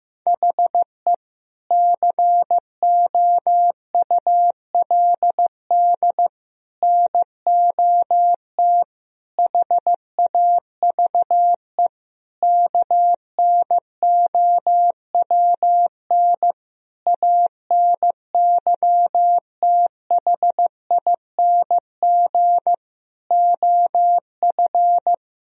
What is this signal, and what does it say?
Morse code, heard as a keyed tone, about 16 wpm, 700 Hz. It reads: HE COULD NOT HAVE KNOWN ANYTHING OF
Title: Sentences from Top 1000 Words - Review Part 3 - 15wpm